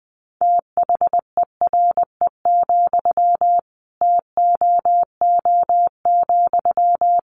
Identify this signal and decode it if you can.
Morse code, heard as a keyed tone, about 20 wpm, 700 Hz. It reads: THERE, TOO,
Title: Sentences from Top 1200 Words 20wpm